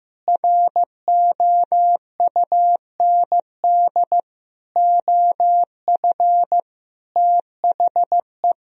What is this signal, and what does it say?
Morse code, heard as a keyed tone, about 15 wpm, 700 Hz. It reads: ROUND OF THE
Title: Sentences from Top 1000 Words - Review Part 4 - 15wpm